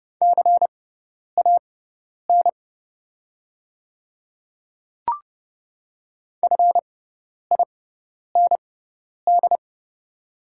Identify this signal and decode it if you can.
Morse code, heard as a keyed tone, about 30 wpm, spaced as for 10 wpm, 690 Hz. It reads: CAN FIND
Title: 6x Character Spacing - Top 200 Words - 30wpm